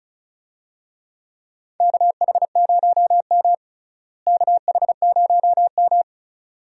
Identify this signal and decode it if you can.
Morse code, heard as a keyed tone, about 34 wpm, 700 Hz. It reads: KH0M KH0M
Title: USA Call Signs - 4 Characters - Repeated 2x - 35wpm